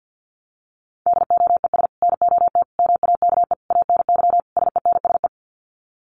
Morse code, heard as a keyed tone, about 50 wpm, 700 Hz. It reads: DOES NOT MAKE ANY SENSE
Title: Sets of 5 Words Encore2 50wpm